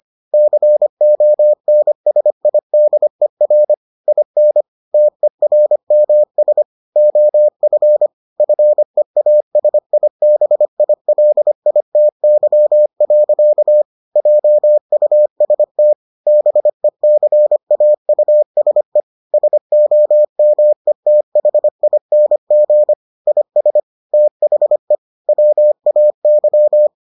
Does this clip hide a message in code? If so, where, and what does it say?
Yes: Morse code from 0.3 to 26.9 s, heard as a keyed tone, about 25 wpm, 600 Hz. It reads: CONSIDER IN TERMS OF FEASIBILITY. JUST BECAUSE SOMETHING IS THE WAY